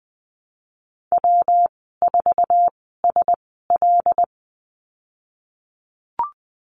Morse code, heard as a keyed tone, about 20 wpm, spaced as for 15 wpm, 700 Hz. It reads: W4SL
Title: 147 Lesson 24g - Letter y - Call Signs 20wpm